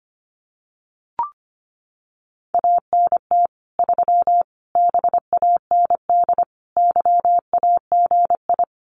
Morse code, heard as a keyed tone, about 25 wpm, 700 Hz. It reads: ANT 3 BAND YAGI
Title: Common QSO Phrases 25wpm